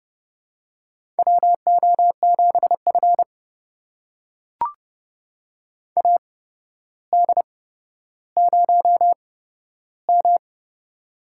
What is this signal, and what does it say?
Morse code, heard as a keyed tone, about 28 wpm, spaced as for 13 wpm, 700 Hz. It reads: WO7F AD0M
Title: USA Call Signs - 4 Characters - 8x Character Spacing - 30wpm